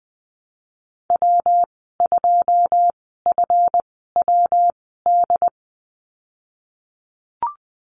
Morse code, heard as a keyed tone, about 20 wpm, spaced as for 14 wpm, 700 Hz. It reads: W2FWD
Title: 186 Lesson 30g - Letter b - Call Signs 20wpm